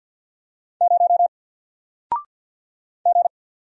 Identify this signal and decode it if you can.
Morse code, heard as a keyed tone, about 48 wpm, 700 Hz. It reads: 0 G
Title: Single Letter-Number 50wpm